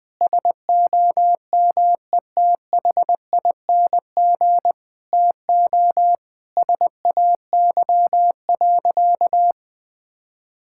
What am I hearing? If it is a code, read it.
Morse code, heard as a keyed tone, about 21 wpm, 700 Hz. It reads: SOMETHING TO SAY.
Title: Sentences with 2-Word Phrases - 20wpm